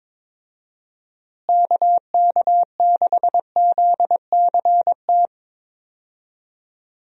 Morse code, heard as a keyed tone, about 22 wpm, 700 Hz. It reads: KK6ZCT
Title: USA Call Signs - 6 Characters - Repeated 3x - 22wpm